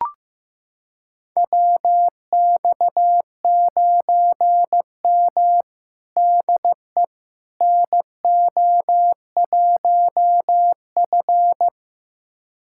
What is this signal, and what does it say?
Morse code, heard as a keyed tone, about 15 wpm, 700 Hz. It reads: WX9M DE NO1F
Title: USA Callsign DE Callsign - 4 Characters - 15wpm